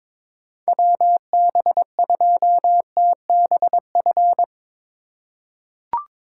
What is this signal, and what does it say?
Morse code, heard as a keyed tone, about 22 wpm, 700 Hz. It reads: WB2TBF
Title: USA Call Signs - 6 Characters - 3x Character Spacing - 22wpm